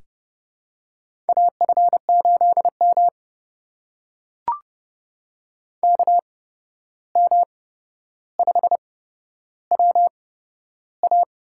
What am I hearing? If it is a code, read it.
Morse code, heard as a keyed tone, about 30 wpm, spaced as for 12 wpm, 700 Hz. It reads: AF8M KM5WA
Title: USA Call Signs - 8x Character Spacing - 30wpm